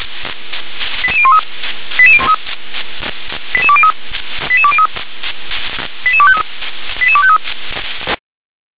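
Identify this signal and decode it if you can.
Five-tone selective call (ZVEI), heard as quick runs of five tones, tones of about 70 ms. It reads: 88013 88203 88203 88253 88253
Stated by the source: ZVEI five-tone selective call received on SDR.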